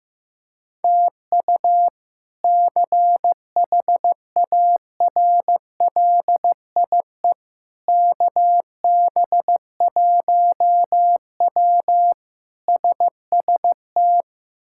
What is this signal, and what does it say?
Morse code, heard as a keyed tone, about 15 wpm, 700 Hz. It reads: TU CHARLIE KB1W SST
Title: K1USN SST QSO Practice - 15wpm